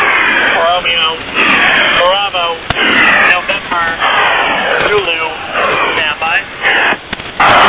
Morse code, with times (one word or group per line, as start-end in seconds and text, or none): none